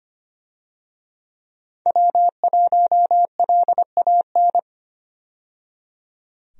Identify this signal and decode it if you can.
Morse code, heard as a keyed tone, about 25 wpm, 700 Hz. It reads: W1LAN
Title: USA Call Signs - 5 Characters - 25wpm